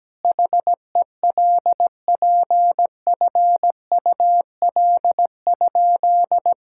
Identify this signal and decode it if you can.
Morse code, heard as a keyed tone, about 17 wpm, 700 Hz. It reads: HELPFUL?
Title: Sentences from Top 1000 Words - Review Part 6 - 17wpm